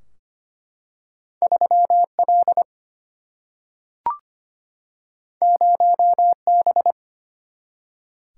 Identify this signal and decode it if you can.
Morse code, heard as a keyed tone, about 25 wpm, 700 Hz. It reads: 3L 0B
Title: Single-Letter Single-Number 25wpm